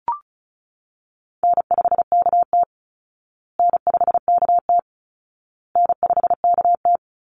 Morse code, heard as a keyed tone, about 34 wpm, 700 Hz. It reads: N5KT N5KT N5KT
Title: Easy Call Signs Repeated 3x - 35wpm